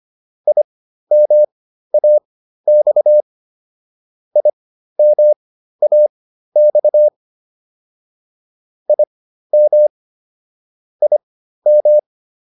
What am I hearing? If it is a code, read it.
Morse code, heard as a keyed tone, about 25 wpm, spaced as for 13 wpm, 600 Hz. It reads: IMAX IMAX IM IM